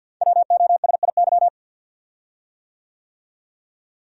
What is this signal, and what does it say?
Morse code, heard as a keyed tone, about 53 wpm, 700 Hz. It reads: WOSIY
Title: Code Groups - 5 Letter:Numbers - 50wpm